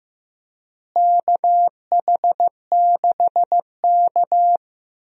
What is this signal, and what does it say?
Morse code, heard as a keyed tone, about 16 wpm, 700 Hz. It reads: KH6K
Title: DXCC-100 Call Signs - 4 Characters - Repeated 3x - 15wpm